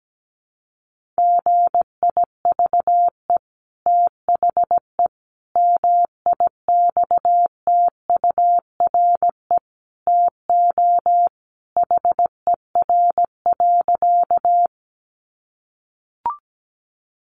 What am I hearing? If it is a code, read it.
Morse code, heard as a keyed tone, about 17 wpm, 700 Hz. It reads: GIVE THE MIXTURE TO HER.
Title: Sentences from Top 1500 Words 17wpm